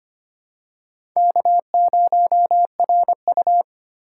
Morse code, heard as a keyed tone, about 25 wpm, 700 Hz. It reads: K0RU
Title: K1USN SST QSO Practice - 25wpm